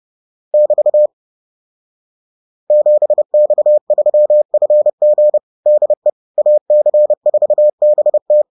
Morse code, heard as a keyed tone, about 30 wpm, 600 Hz. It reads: <BT> 7X3FG DE AC4BT